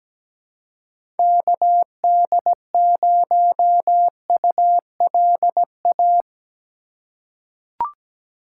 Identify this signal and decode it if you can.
Morse code, heard as a keyed tone, about 17 wpm, 700 Hz. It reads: KD0ULA
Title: USA Call Signs - 2x Character Spacing - 17wpm